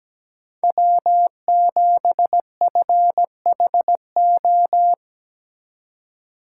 Morse code, heard as a keyed tone, about 17 wpm, 700 Hz. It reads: W7FHO
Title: USA Call Signs - 5 Characters - 4x Character Spacing - 17wpm